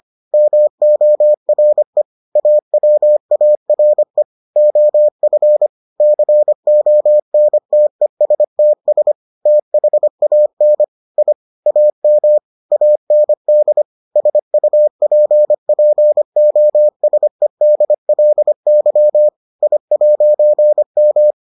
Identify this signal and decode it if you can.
Morse code, heard as a keyed tone, about 25 wpm, 600 Hz. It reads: MORE AWARE OF CONTESTS THAN I AM AND SUPPOSEDLY I'M